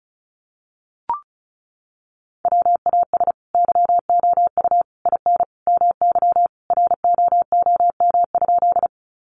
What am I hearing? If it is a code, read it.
Morse code, heard as a keyed tone, about 34 wpm, 700 Hz. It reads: WAS YOU IN MY ROOM?
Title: Sentences from Top 300 Words 35wpm